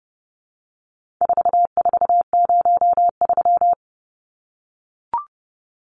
Morse code, heard as a keyed tone, about 30 wpm, 700 Hz. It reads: 4403
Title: Numbers - 4-Digit 30wpm